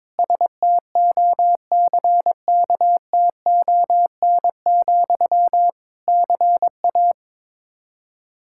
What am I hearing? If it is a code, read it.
Morse code, heard as a keyed tone, about 22 wpm, 700 Hz. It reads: STOCKTON, CA
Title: K1USN SST QSO Practice 22wpm